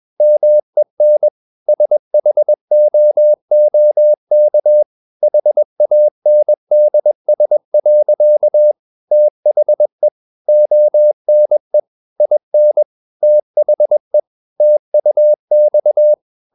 Morse code, heard as a keyed tone, about 21 wpm, 600 Hz. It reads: MEN SHOOK HANDS. THE ONE IN THE TUX
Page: CW med Gnister